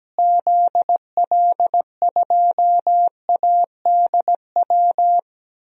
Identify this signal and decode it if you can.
Morse code, heard as a keyed tone, about 17 wpm, 700 Hz. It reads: ZL2ADW
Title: DXCC-100 Call Signs - 6 Characters - Repeated 3x - 17wpm